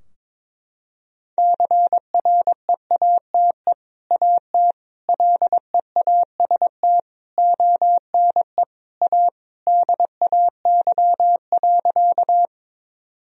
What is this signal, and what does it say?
Morse code, heard as a keyed tone, about 22 wpm, 700 Hz. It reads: CREATE AT LEAST ONE A DAY.
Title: Sentences from Top 1000 Words - Review Part 3 - 22wpm